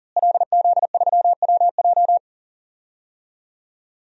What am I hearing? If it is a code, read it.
Morse code, heard as a keyed tone, about 37 wpm, 700 Hz. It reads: LZ3WJ
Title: DXCC-100 Call Signs - 5 Characters - Repeated 2x - 40wpm